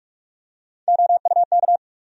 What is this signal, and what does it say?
Morse code, heard as a keyed tone, about 47 wpm, 700 Hz. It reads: OUK
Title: USA Call Sign Postfixes - 3 Letters 45wpm